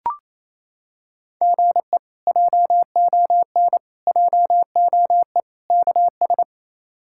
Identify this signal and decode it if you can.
Morse code, heard as a keyed tone, about 28 wpm, 700 Hz. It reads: GE JON JOE KS
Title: K1USN SST QSO Practice - 28wpm